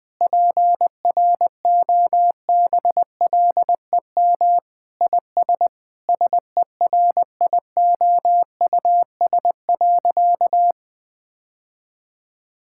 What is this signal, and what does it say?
Morse code, heard as a keyed tone, about 21 wpm, 700 Hz. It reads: PROBLEM IS SERIOUS.
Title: Sentences with 3-Word Phrases - 20wpm